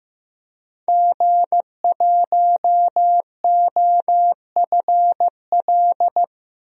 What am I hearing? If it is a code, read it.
Morse code, heard as a keyed tone, about 16 wpm, 700 Hz. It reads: G1OFL
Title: DXCC-100 Call Signs - 5 Characters - 15wpm